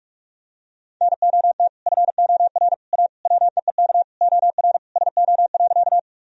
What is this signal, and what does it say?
Morse code, heard as a keyed tone, about 43 wpm, 700 Hz. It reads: NOT FOR A WEEK OR SO.
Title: Sentences from Top 1000 Words - Review Part 2 - 45wpm